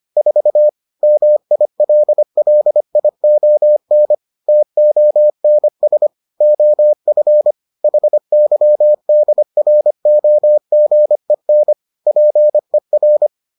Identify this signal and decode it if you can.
Morse code, heard as a keyed tone, about 25 wpm, 600 Hz. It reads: 4 MILLION TONS OF HYDROGEN PER